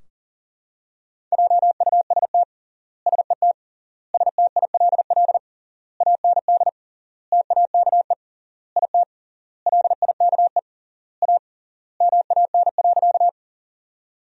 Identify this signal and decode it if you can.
Morse code, heard as a keyed tone, about 40 wpm, 700 Hz. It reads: JUST SET STILL AND TAKE IT LIKE A MAN.
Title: 3x Word-Spacing - Sentences from Top 300 Words - 40wpm